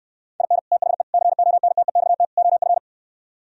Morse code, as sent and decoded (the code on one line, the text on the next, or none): .. -. / - .... / -.-. --- -. - . -..- - / --- ..-.
IN THE CONTEXT OF